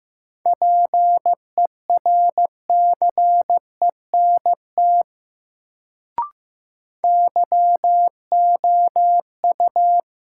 Morse code, heard as a keyed tone, about 15 wpm, 700 Hz. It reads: PERCENT YOU
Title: Sets of 2 Words - New Word Drill - 15wpm